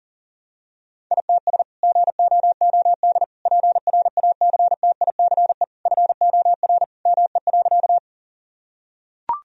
Morse code, heard as a keyed tone, about 40 wpm, 700 Hz. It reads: ITS GOOD PRACTICE FOR ME.
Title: Sentences from Top 1000 Words - Review Part 4 - 40wpm